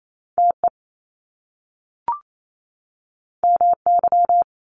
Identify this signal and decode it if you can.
Morse code, heard as a keyed tone, about 28 wpm, 700 Hz. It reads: TE MY
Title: Top Two Letter Combinations in Words 28wpm